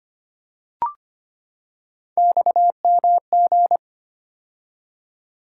Morse code, heard as a keyed tone, about 25 wpm, 700 Hz. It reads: XMG
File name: Three Letter-Number Combinations - 25wpm